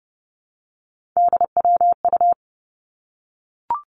Morse code, heard as a keyed tone, about 30 wpm, 700 Hz. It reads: DWU
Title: Three Challenging Letter-Number Combinations 30wpm